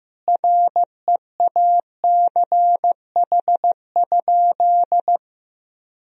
Morse code, heard as a keyed tone, about 16 wpm, 700 Hz. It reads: REACH?
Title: Sentences from Top 1000 Words - Review Part 4 - 15wpm